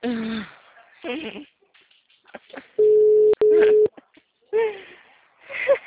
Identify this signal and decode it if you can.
Ringback tone (Europe, 425 Hz), with speech.